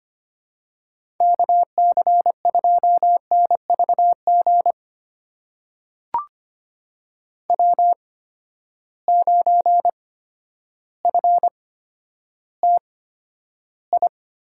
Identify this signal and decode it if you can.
Morse code, heard as a keyed tone, about 25 wpm, spaced as for 11 wpm, 700 Hz. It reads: KC2NVG W9FTI
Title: USA Call Signs - 8x Character Spacing - 25wpm